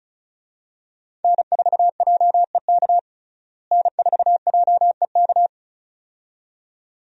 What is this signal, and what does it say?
Morse code, heard as a keyed tone, about 34 wpm, 700 Hz. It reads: N4JEK N4JEK
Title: USA Call Signs - 5 Characters - Repeated 2x - 35wpm